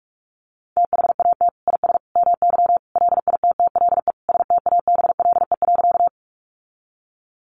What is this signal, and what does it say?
Morse code, heard as a keyed tone, about 44 wpm, 700 Hz. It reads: THAT IS MY LITTLE STABLE.
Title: Sentences from Top 1500 Words 45wpm